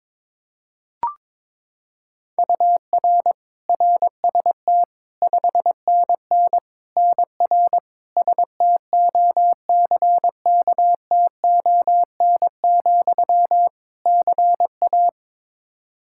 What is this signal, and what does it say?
Morse code, heard as a keyed tone, about 22 wpm, 700 Hz. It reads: UR RST 5NN NR STOCKTON, CA
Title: Practice QSO - Exchange 1 - Beginning - 22wpm